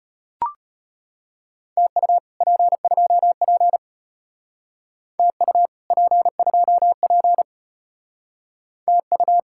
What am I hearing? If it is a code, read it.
Morse code, heard as a keyed tone, about 33 wpm, 700 Hz. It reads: TU P2P TU P2P TU